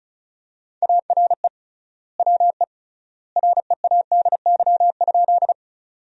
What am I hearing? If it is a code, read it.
Morse code, heard as a keyed tone, about 34 wpm, 700 Hz. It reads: ARE WE READY?
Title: 3x Word-Spacing - Sentences from Top 400 Words - 35wpm